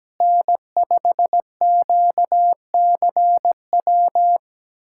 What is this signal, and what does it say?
Morse code, heard as a keyed tone, about 17 wpm, 700 Hz. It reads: N5QCW
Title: USA Callsign DE Callsign - 5 Characters - 17wpm